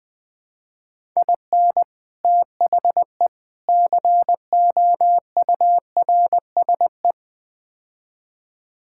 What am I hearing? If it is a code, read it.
Morse code, heard as a keyed tone, about 20 wpm, 700 Hz. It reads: IN THE COURSE
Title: Build A Phrase - 4-Word Phrases - 20wpm